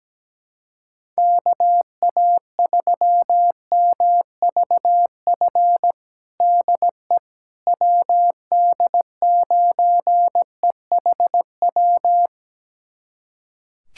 Morse code, heard as a keyed tone, about 17 wpm, 700 Hz. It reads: KA3MVF DE WD9EHW
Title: USA Callsign DE Callsign 17wpm